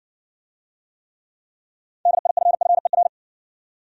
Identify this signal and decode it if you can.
Morse code, heard as a keyed tone, about 57 wpm, 700 Hz. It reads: DIFFER